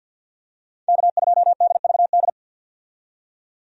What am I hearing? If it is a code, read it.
Morse code, heard as a keyed tone, about 49 wpm, 700 Hz. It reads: K2DVD
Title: USA Call Signs - 5 Characters - 50wpm